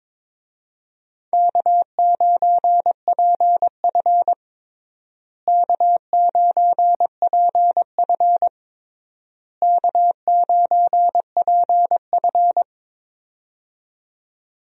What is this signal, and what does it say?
Morse code, heard as a keyed tone, about 22 wpm, 700 Hz. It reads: K9PF K9PF K9PF
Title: USA Call Signs - 4 Characters - Repeated 3x - 22wpm